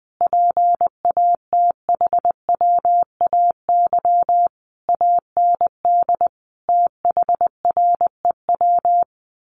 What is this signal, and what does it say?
Morse code, heard as a keyed tone, about 20 wpm, 700 Hz. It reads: PATHWAY AND THREW
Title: A Sound Of Thunder 20wpm